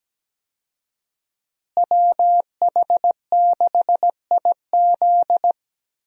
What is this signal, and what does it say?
Morse code, heard as a keyed tone, about 17 wpm, 700 Hz. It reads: WH6IZ
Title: USA Call Signs - 5 Characters - Repeated 2x - 17wpm